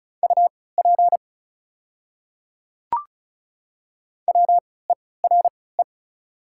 Morse code, heard as a keyed tone, about 36 wpm, spaced as for 20 wpm, 700 Hz. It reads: UP WERE
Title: 3x Character Spacing - Top 100 Words - 35wpm